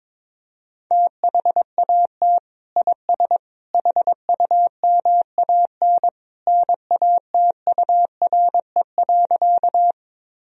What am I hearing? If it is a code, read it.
Morse code, heard as a keyed tone, about 22 wpm, 700 Hz. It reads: THAT IS HUMAN NATURE.
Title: Sentences from Top 1000 Words - Review Part 3 - 22wpm